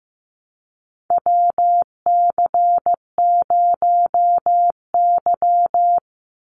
Morse code, heard as a keyed tone, about 15 wpm, 700 Hz. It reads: WC0Y
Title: NA SOTA Chaser Callsigns 15wpm